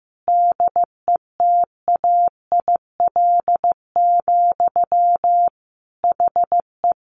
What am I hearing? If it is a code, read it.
Morse code, heard as a keyed tone, about 15 wpm, 700 Hz. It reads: DETAIL, HE
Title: Sentences from Top 1500 Words 15wpm